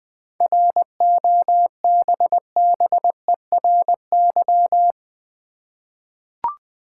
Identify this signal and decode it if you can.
Morse code, heard as a keyed tone, about 20 wpm, 700 Hz. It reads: ROBBERY
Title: 7 Letter Words - First 3 Letters Given - 20wpm